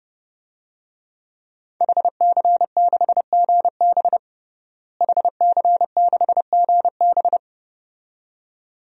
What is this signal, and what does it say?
Morse code, heard as a keyed tone, about 32 wpm, 700 Hz. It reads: HC6GB HC6GB
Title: DXCC-100 Call Signs - 5 Characters - Repeated 2x - 30wpm